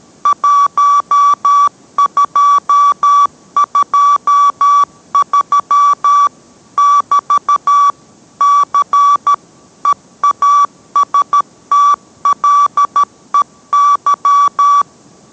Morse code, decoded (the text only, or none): 1223 <BT> C EASTLEY